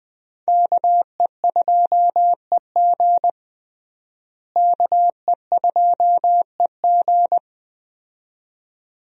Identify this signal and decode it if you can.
Morse code, heard as a keyed tone, about 20 wpm, 700 Hz. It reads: KE2EG KE2EG
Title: USA Call Signs - 5 Characters - Repeated 2x - 20wpm